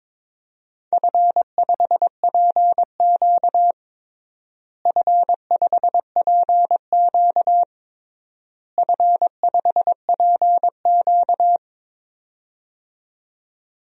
Morse code, heard as a keyed tone, about 22 wpm, 700 Hz. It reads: F5PQ F5PQ F5PQ
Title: DXCC-100 Call Signs - 4 Characters - Repeated 3x - 22wpm